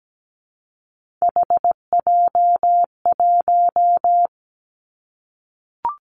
Morse code, heard as a keyed tone, about 17 wpm, 700 Hz. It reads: HJ1
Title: Three Challenging Letter-Number Combinations 17wpm